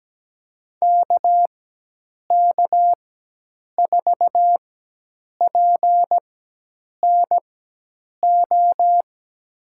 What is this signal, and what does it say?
Morse code, heard as a keyed tone, about 17 wpm, spaced as for 8 wpm, 700 Hz. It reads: KK4PNO
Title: USA Call Signs - 6 Characters - 4x Character Spacing - 17wpm